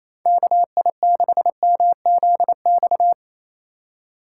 Morse code, heard as a keyed tone, about 35 wpm, 700 Hz. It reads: KI6MZX